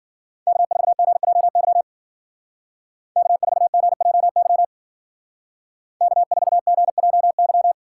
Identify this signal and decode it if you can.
Morse code, heard as a keyed tone, about 51 wpm, 700 Hz. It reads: K4GJY K4GJY K4GJY